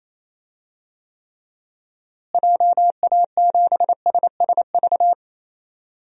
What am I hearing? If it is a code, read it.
Morse code, heard as a keyed tone, about 28 wpm, 700 Hz. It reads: JA7SSV
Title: DXCC-100 Call Signs - 6 Characters - Repeated 3x - 28wpm